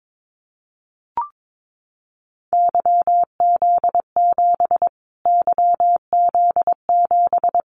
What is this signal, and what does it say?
Morse code, heard as a keyed tone, about 22 wpm, 700 Hz. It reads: YZ7 YZ7
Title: Callsign Prefixes for Top 25 DXCC - Repeated 3x - 22wpm